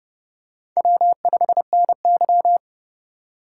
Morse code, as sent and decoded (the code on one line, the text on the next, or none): .-- ..... -. -.--
W5NY